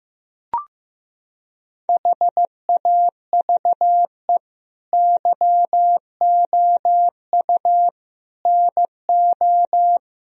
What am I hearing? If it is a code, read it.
Morse code, heard as a keyed tone, about 15 wpm, 700 Hz. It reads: HAVE YOU NO
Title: Sentences from Top 1000 Words - Review Part 1 - 15wpm